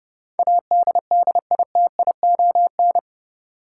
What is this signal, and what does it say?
Morse code, heard as a keyed tone, about 30 wpm, 700 Hz. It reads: ADDITION
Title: Sets of 3 Words - New Word Drill - 30wpm